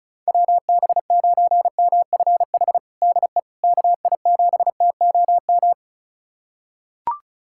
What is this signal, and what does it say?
Morse code, heard as a keyed tone, about 36 wpm, 690 Hz. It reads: WB9MFH DE KI7TOM
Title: USA Callsign DE Callsign - 6 Characters - 35wpm